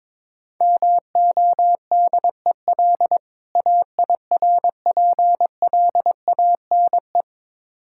Morse code, heard as a keyed tone, about 22 wpm, 690 Hz. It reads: MODEL AIRPLANE
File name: Difficult Words with Hints - 22wpm